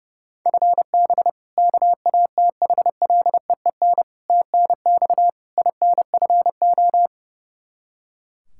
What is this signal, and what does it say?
Morse code, heard as a keyed tone, about 31 wpm, 700 Hz. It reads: FB KATHLEEN TNX INFO
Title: Practice QSO - Exchange 3 - More Info - 30wpm